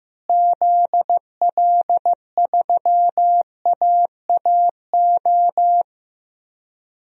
Morse code, heard as a keyed tone, about 15 wpm, 700 Hz. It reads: ZL3AAO
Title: DXCC-100 Call Signs - 6 Characters - Repeated 2x - 15wpm